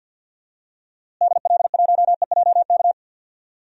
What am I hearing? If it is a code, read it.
Morse code, heard as a keyed tone, about 48 wpm, 700 Hz. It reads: DL1EJK
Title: DXCC-100 Call Signs - 6 Characters - Repeated 3x - 50wpm